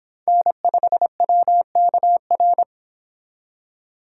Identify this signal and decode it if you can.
Morse code, heard as a keyed tone, about 26 wpm, 700 Hz. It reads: N5WKR